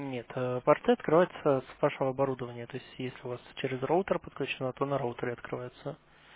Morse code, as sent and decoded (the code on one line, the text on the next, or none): none